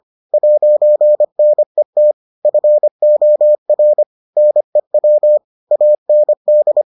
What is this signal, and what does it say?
Morse code, heard as a keyed tone, about 25 wpm, 600 Hz. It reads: 'NET FOR NEW AND